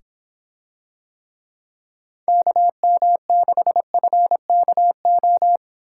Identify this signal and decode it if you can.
Morse code, heard as a keyed tone, about 24 wpm, 700 Hz. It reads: KM6FKO